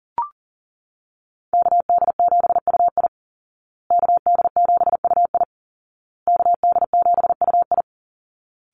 Morse code, heard as a keyed tone, about 40 wpm, 700 Hz. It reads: KD7UI KD7UI KD7UI
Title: Call Signs Repeated 2x - 40wpm